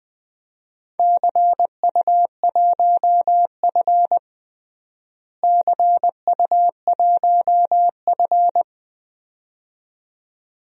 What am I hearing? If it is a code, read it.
Morse code, heard as a keyed tone, about 20 wpm, 700 Hz. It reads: CU1F CU1F
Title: DXCC-100 Call Signs - 4 Characters - Repeated 2x - 20wpm